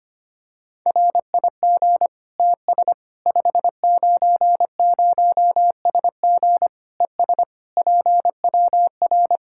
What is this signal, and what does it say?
Morse code, heard as a keyed tone, about 25 wpm, 700 Hz. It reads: RIG TS 590SG ES PWR
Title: Practice QSO - Exchange 2 - Info - 25wpm